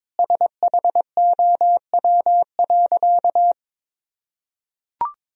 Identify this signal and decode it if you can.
Morse code, heard as a keyed tone, about 22 wpm, 700 Hz. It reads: SHOW.
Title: Sentences from Top 1000 Words - Review Part 3 - 22wpm